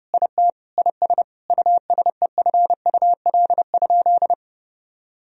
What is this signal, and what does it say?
Morse code, heard as a keyed tone, about 32 wpm, 700 Hz. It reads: IT IS USEFUL?
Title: Sentences from Top 1000 Words - Review Part 4 - 30wpm